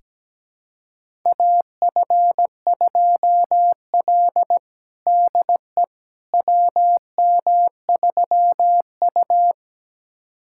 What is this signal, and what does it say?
Morse code, heard as a keyed tone, about 17 wpm, 700 Hz. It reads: AF2L DE WM3U
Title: USA Callsign DE Callsign - 4 Characters - 17wpm